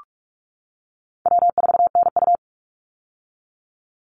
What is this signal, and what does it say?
Morse code, heard as a keyed tone, about 44 wpm, 700 Hz. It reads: W4NU
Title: Easy Call Signs - 45wpm